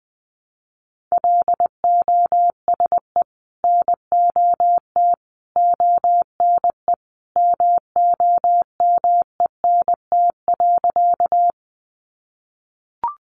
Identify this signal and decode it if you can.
Morse code, heard as a keyed tone, about 20 wpm, 700 Hz. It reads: LOSE NOT ONE MOMENT.
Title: Sentences from Top 1100 Words 20wpm